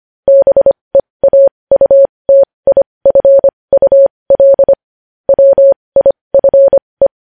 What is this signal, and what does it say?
Morse code, heard as a keyed tone, about 25 wpm, spaced as for 20 wpm, 550 Hz. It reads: BEAUTIFUL WIFE
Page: news-headlines-in-morse-code-at-20-wpm